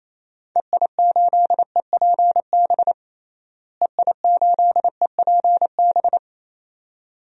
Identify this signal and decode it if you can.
Morse code, heard as a keyed tone, about 28 wpm, 700 Hz. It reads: EI8EPB EI8EPB
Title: DXCC-100 Call Signs - 6 Characters - Repeated 2x - 28wpm